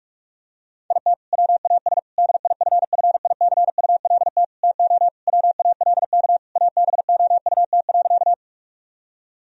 Morse code, heard as a keyed tone, about 47 wpm, 700 Hz. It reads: IT WAS DIFFICULT TO WALK ABOUT.
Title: Sentences from Top 1000 Words - Review Part 1 - 45wpm